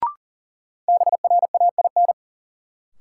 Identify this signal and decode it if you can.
Morse code, heard as a keyed tone, about 40 wpm, 700 Hz. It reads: BRAIN
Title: Sentences from Top 1100 Words - New Word Drill Alt - 40wpm